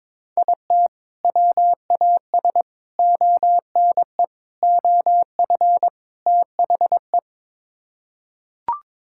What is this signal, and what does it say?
Morse code, heard as a keyed tone, about 22 wpm, 700 Hz. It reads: IT WAS ONE OF THE
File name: practice / 22wpm / Build A Phrase - 5-Word Phrases - 22wpm